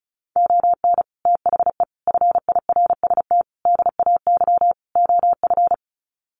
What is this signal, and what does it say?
Morse code, heard as a keyed tone, about 34 wpm, 700 Hz. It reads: ON THE FIRST DAY OF
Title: Sets of 5 Words Encore 35wpm